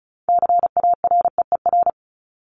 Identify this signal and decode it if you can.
Morse code, heard as a keyed tone, about 34 wpm, 700 Hz. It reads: CAREER
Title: Top 751-1000 Words 35wpm